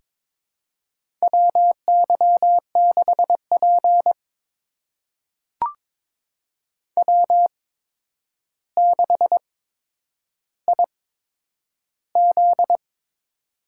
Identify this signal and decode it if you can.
Morse code, heard as a keyed tone, about 21 wpm, spaced as for 9 wpm, 700 Hz. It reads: WY6P W6IZ
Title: USA Call Signs - 4 Characters - 8x Character Spacing - 22wpm